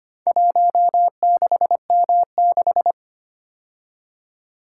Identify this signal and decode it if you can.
Morse code, heard as a keyed tone, about 25 wpm, 700 Hz. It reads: 16M6
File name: practice / 25wpm / Four Letter-Number Combinations - 25wpm